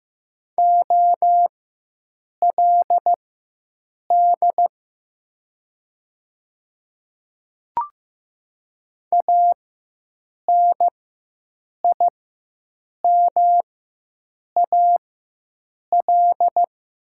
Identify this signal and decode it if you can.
Morse code, heard as a keyed tone, about 15 wpm, spaced as for 7 wpm, 700 Hz. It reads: OLD ANIMAL
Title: 4x Character Spacing - Top 200 Words - 15wpm